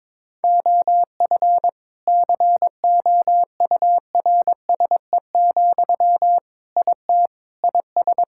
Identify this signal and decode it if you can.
Morse code, heard as a keyed tone, about 22 wpm, 700 Hz. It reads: OF COURSE, IT IS
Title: Sentences from Top 1600 Words 22wpm